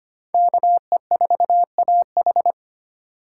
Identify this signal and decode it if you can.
Morse code, heard as a keyed tone, about 32 wpm, 700 Hz. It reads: KE4AH